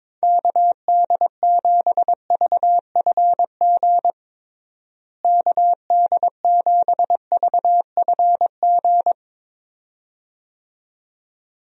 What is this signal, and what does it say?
Morse code, heard as a keyed tone, about 22 wpm, 700 Hz. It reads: KD7VFG KD7VFG
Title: USA Call Signs - 6 Characters - Repeated 2x - 22wpm